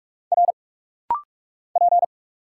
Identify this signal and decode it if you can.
Morse code, heard as a keyed tone, about 46 wpm, 700 Hz. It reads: R P
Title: Single Letters - Rapid-Fire 45wpm